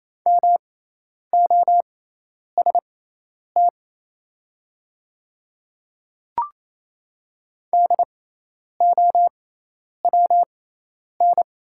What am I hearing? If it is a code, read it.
Morse code, heard as a keyed tone, about 28 wpm, spaced as for 10 wpm, 700 Hz. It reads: MOST DOWN
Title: 6x Character Spacing - Top 300 Words - 28wpm